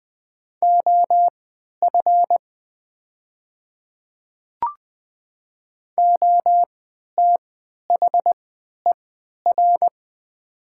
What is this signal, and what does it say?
Morse code, heard as a keyed tone, about 20 wpm, spaced as for 11 wpm, 700 Hz. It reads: OF OTHER
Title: 3x Character Spacing - Top 100 Words - 20wpm